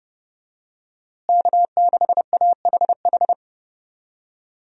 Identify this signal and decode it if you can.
Morse code, heard as a keyed tone, about 31 wpm, 700 Hz. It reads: K6AHH
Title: USA Call Signs - 5 Characters - Repeated 3x - 30wpm